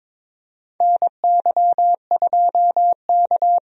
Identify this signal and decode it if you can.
Morse code, heard as a keyed tone, about 21 wpm, 700 Hz. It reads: NY2K
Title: USA Call Signs - 4 Characters - Repeated 3x - 22wpm